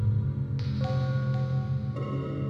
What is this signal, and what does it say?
There is no code.